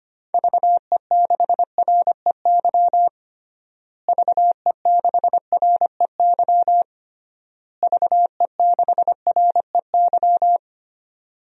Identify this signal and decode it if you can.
Morse code, heard as a keyed tone, about 25 wpm, 700 Hz. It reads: VE6REY VE6REY VE6REY
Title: DXCC-100 Call Signs - 6 Characters - Repeated 3x - 25wpm